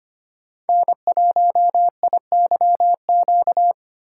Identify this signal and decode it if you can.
Morse code, heard as a keyed tone, about 25 wpm, 700 Hz. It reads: N1IYQ
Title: USA Call Signs - 5 Characters - 4x Character Spacing - 25wpm